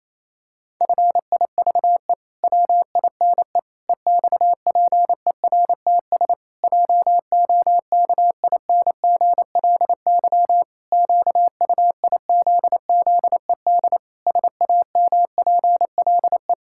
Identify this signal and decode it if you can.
Morse code, heard as a keyed tone, about 28 wpm, 700 Hz. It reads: FIVE WINE EXPERTS JOKINGLY QUIZZED SAMPLE
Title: Pangram Challenge v2 - 28wpm